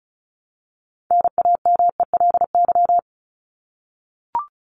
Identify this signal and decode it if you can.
Morse code, heard as a keyed tone, about 34 wpm, 700 Hz. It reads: NAMELY
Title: LY Suffix - Top 100 Words with Base Words 35wpm